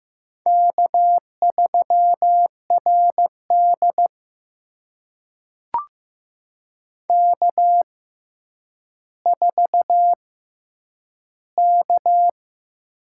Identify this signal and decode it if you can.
Morse code, heard as a keyed tone, about 15 wpm, spaced as for 9 wpm, 700 Hz. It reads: K3RD K4K
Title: USA Call Signs - 4 Characters - 6x Character Spacing - 15wpm